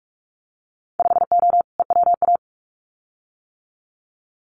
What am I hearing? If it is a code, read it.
Morse code, heard as a keyed tone, about 44 wpm, 700 Hz. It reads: 5O EWA
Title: Winter Field Day Exchanges Encore 45wpm